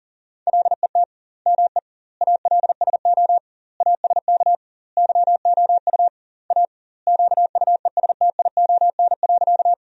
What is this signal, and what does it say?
Morse code, heard as a keyed tone, about 40 wpm, 700 Hz. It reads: LET ME ALSO ASK YOU A QUESTION.
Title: 2x Word-Spacing - Sentences from Top 400 Words - 40wpm